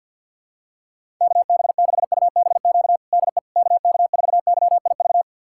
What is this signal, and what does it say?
Morse code, heard as a keyed tone, about 53 wpm, 700 Hz. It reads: KB6UBQ DE KK4YIV
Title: USA Callsign DE Callsign - 6 Characters - 50wpm